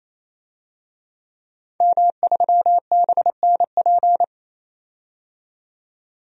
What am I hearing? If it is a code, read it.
Morse code, heard as a keyed tone, about 28 wpm, 700 Hz. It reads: M3BNP
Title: DXCC-100 Call Signs - 5 Characters - 28wpm